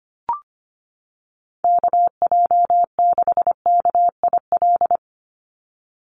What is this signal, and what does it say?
Morse code, heard as a keyed tone, about 25 wpm, 700 Hz. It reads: KJ6KIL
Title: USA Call Signs 25wpm